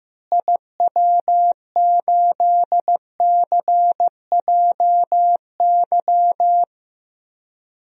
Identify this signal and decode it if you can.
Morse code, heard as a keyed tone, about 15 wpm, 700 Hz. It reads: IW8CJY
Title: DXCC-100 Call Signs - 6 Characters - Repeated 2x - 15wpm